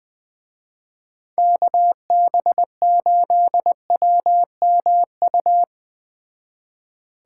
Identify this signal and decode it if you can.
Morse code, heard as a keyed tone, about 20 wpm, 700 Hz. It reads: KB8WMU
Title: Speed Racing - USA 2x3 Call Signs 17wpm